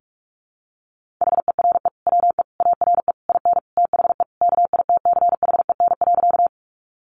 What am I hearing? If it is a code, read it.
Morse code, heard as a keyed tone, about 44 wpm, 700 Hz. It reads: HERE WE ARE IN THE KITCHEN.
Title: Sentences from Top 1400 Words 45wpm